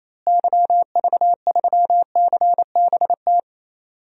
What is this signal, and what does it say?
Morse code, heard as a keyed tone, about 28 wpm, 700 Hz. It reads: YV3CBT
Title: DXCC-100 Call Signs - 6 Characters - Repeated 3x - 28wpm